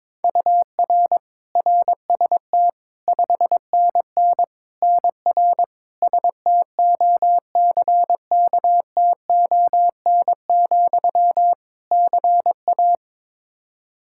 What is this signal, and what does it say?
Morse code, heard as a keyed tone, about 22 wpm, 700 Hz. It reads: UR RST 5NN NR STOCKTON, CA
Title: Practice QSO - Exchange 1 - Beginning - 22wpm